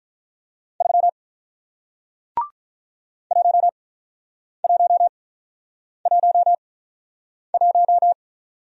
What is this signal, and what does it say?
Morse code, heard as a keyed tone, about 48 wpm, 700 Hz. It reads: 3 1 1 1 1